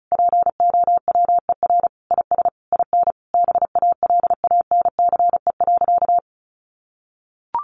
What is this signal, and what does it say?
Morse code, heard as a keyed tone, about 34 wpm, 700 Hz. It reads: POWER IS IN BALANCE.
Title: Sentences from Top 900 Words 35wpm